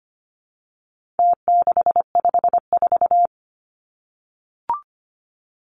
Morse code, heard as a keyed tone, about 25 wpm, 700 Hz. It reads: T654
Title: Numbers - Contest Style - 4-Digit 25wpm